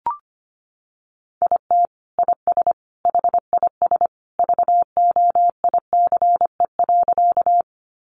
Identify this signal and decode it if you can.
Morse code, heard as a keyed tone, about 25 wpm, 700 Hz. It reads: IT IS HIS VOICE.
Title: Sentences from Top 500 Words 25wpm